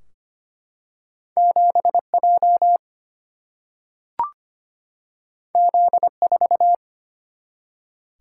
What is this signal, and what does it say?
Morse code, heard as a keyed tone, about 25 wpm, 700 Hz. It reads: 7J Z4
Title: Single-Letter Single-Number 25wpm